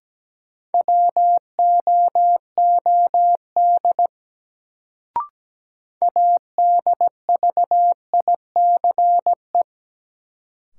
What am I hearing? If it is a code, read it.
Morse code, heard as a keyed tone, about 17 wpm, 700 Hz. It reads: WOOD ADVICE
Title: Sentences from Top 1100 Words - New Word Drill Alt - 17wpm